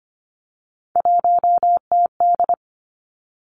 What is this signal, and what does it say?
Morse code, heard as a keyed tone, about 25 wpm, 700 Hz. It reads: 1TD
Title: Three Challenging Letter-Number Combinations 25wpm